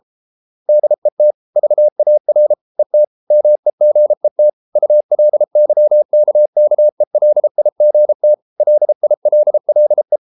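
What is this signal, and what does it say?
Morse code, heard as a keyed tone, about 33 wpm, 600 Hz. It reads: DET VAR ET MEGET ULYKKELIGT LILLE
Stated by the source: Morse code at 33 wpm, QRQ.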